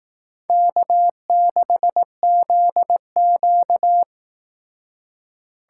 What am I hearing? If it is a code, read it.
Morse code, heard as a keyed tone, about 18 wpm, 700 Hz. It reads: K6ZQ
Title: Speed Racing - USA 1x2 Call Signs 15wpm